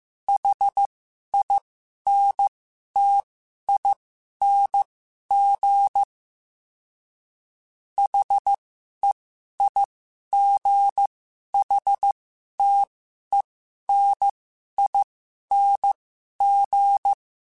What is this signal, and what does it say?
Morse code, heard as a keyed tone, about 15 wpm, spaced as for 11 wpm, 800 Hz. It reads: HINTING HEIGHTENING
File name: Special INGs-mp3s / Special INGs-mp3s / Sngl Ing @10